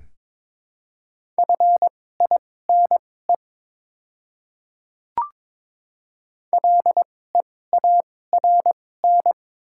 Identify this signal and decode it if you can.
Morse code, heard as a keyed tone, about 22 wpm, spaced as for 16 wpm, 700 Hz. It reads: FINE LEARN
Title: 2x Character Spacing - Top 500 Words - 22wpm